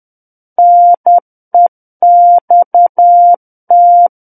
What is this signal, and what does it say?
Morse code, heard as a keyed tone, about 10 wpm, 700 Hz. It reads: NEXT
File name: Book One / 01-Chapter 1 - 10 WPM